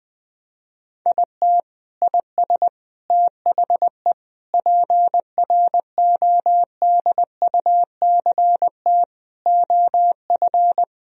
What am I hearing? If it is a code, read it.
Morse code, heard as a keyed tone, about 21 wpm, 700 Hz. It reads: IT IS THE PRODUCT OF
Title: Sentences from Top 1000 Words - Review Part 2 - 20wpm